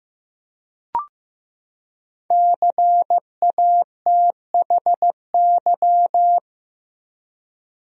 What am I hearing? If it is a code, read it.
Morse code, heard as a keyed tone, about 15 wpm, 700 Hz. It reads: CATHY
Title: USA Common Names - Max 5 Letters - 15wpm